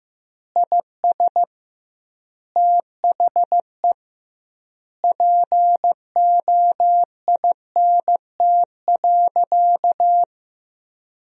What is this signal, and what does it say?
Morse code, heard as a keyed tone, about 15 wpm, 700 Hz. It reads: IS THE POINT.
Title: 2x Word-Spacing - Sentences from Top 200 Words - 15wpm